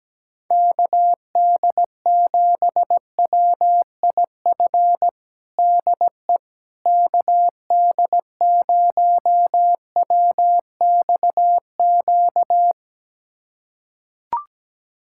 Morse code, heard as a keyed tone, about 17 wpm, 700 Hz. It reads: KD7WIF DE KD0WXQ
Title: USA Callsign DE Callsign - 6 Characters - 17wpm